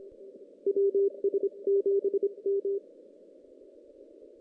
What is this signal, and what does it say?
Morse code, heard as a keyed tone, about 27 wpm, 410 Hz. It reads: WS7M
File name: 2016 JAN CW NAQP / 2016 Jan naqp k5ot - 0-5